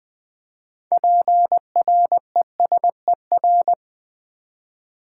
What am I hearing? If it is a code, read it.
Morse code, heard as a keyed tone, about 20 wpm, 700 Hz. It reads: PRESER
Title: Build A Word - 6 Letters At A Time - 12-Letter Words - 20wpm